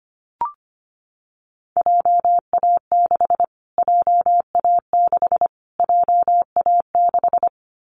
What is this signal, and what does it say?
Morse code, heard as a keyed tone, about 25 wpm, 700 Hz. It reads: JA6 JA6 JA6
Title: Callsign Prefixes for Top 25 DXCC - Repeated 3x - 25wpm